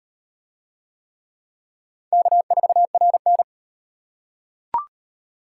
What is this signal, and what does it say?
Morse code, heard as a keyed tone, about 35 wpm, 700 Hz. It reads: K4RN